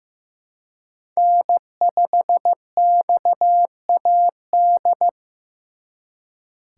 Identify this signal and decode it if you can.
Morse code, heard as a keyed tone, about 18 wpm, 700 Hz. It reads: N5XAD
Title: Speed Racing - USA 1x3 Call Signs 15wpm